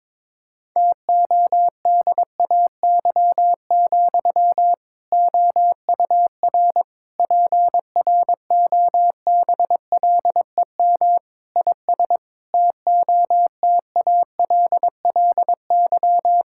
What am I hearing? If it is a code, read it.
Morse code, heard as a keyed tone, about 22 wpm, 700 Hz. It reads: TODAY, OUR PROBLEM IS TOTALLY
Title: Sentences from Top 1600 Words 22wpm